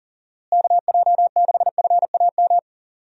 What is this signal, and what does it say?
Morse code, heard as a keyed tone, about 40 wpm, 700 Hz. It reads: KJ6FAM
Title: USA Call Signs - 6 Characters - Repeated 2x - 40wpm